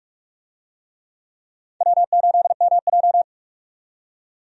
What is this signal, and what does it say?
Morse code, heard as a keyed tone, about 46 wpm, 700 Hz. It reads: W8MJ
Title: USA Call Signs - 4 Characters - 45wpm